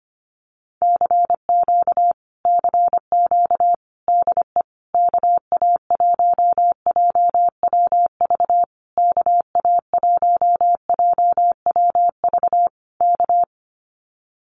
Morse code, heard as a keyed tone, about 25 wpm, 700 Hz. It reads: CQ CQ DE KA1JWV KA1JWV K
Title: Common QSO Phrases Encore 25wpm